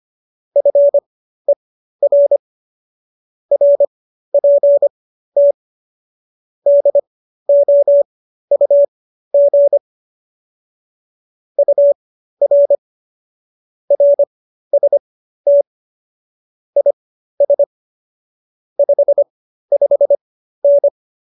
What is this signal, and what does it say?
Morse code, heard as a keyed tone, about 25 wpm, spaced as for 13 wpm, 570 Hz. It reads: FER RPT DOUG UR RST IS 55N